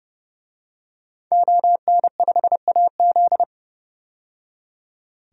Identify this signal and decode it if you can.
Morse code, heard as a keyed tone, about 32 wpm, 700 Hz. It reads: ON5AZ
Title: DXCC-100 Call Signs - 5 Characters - 30wpm